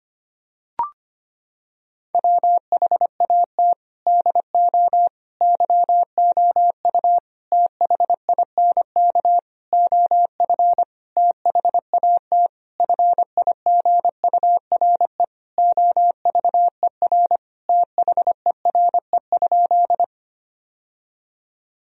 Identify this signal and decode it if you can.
Morse code, heard as a keyed tone, about 25 wpm, 700 Hz. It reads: WHAT DO YOU THINK OF THAT FIGURE OVER THERE?
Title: Sentences from Top 1000 Words - Review Part 2 - 25wpm